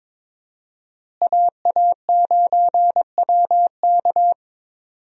Morse code, heard as a keyed tone, about 22 wpm, 700 Hz. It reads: AA9WK
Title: USA Call Signs - 5 Characters - Repeated 3x - 22wpm